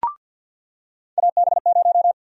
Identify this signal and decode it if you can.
Morse code, heard as a keyed tone, about 47 wpm, 700 Hz. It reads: AB0
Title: USA Call Sign Prefixes - 50wpm